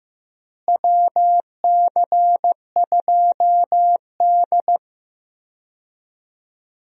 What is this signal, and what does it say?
Morse code, heard as a keyed tone, about 15 wpm, 700 Hz. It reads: WC2D
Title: USA Call Signs - 4 Characters - Repeated 3x - 15wpm